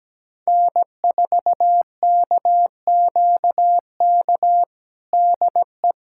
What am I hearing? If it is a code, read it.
Morse code, heard as a keyed tone, about 17 wpm, 700 Hz. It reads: N4KQK DE
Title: USA Callsign DE Callsign - 5 Characters - 17wpm